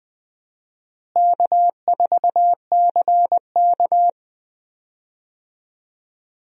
Morse code, heard as a keyed tone, about 21 wpm, 700 Hz. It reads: K4CK
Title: USA Call Signs - 4 Characters - Repeated 2x - 20wpm